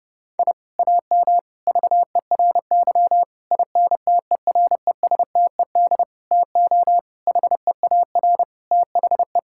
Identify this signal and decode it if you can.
Morse code, heard as a keyed tone, about 30 wpm, 700 Hz. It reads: I AM VERY INTERESTED TO HEAR THE
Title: Sentences from Top 1000 Words - Review Part 6 - 30wpm